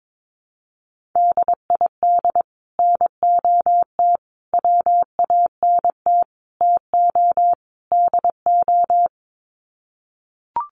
Morse code, heard as a keyed tone, about 22 wpm, 700 Hz. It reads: DID NOT WANT TO DO
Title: Sets of 5 Words Encore 22wpm